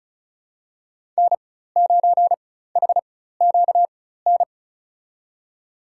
Morse code, heard as a keyed tone, about 34 wpm, spaced as for 16 wpm, 700 Hz. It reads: N9HQN
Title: USA Call Signs - 5 Characters - 4x Character Spacing - 35wpm